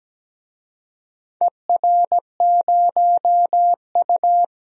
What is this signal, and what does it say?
Morse code, heard as a keyed tone, about 17 wpm, 700 Hz. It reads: ER0U
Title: DXCC-100 Call Signs - 4 Characters - Repeated 2x - 17wpm